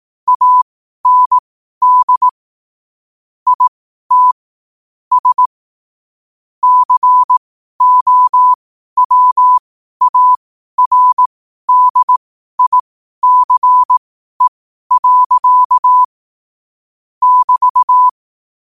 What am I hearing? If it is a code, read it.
Morse code, heard as a keyed tone, about 18 wpm, spaced as for 12 wpm, 1000 Hz. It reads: AND IT S COWARDICE. <BT>
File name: qotdfiles / 12wpm